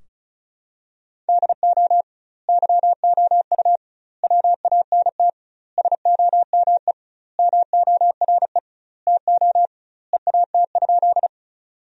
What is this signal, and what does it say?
Morse code, heard as a keyed tone, about 34 wpm, 700 Hz. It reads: DO YOU WANT SOME MORE TO EAT?
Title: 2x Word-Spacing - Sentences from Top 300 Words - 35wpm